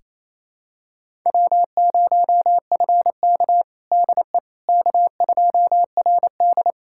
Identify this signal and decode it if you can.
Morse code, heard as a keyed tone, about 28 wpm, 700 Hz. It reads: W0FK DE K2RD
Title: USA Callsign DE Callsign - 4 Characters - 28wpm